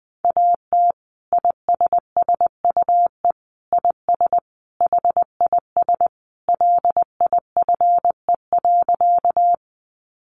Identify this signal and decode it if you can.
Morse code, heard as a keyed tone, about 20 wpm, 700 Hz. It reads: AT ISSUE IS HIS LIFE.
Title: Sentences from Top 900 Words 20wpm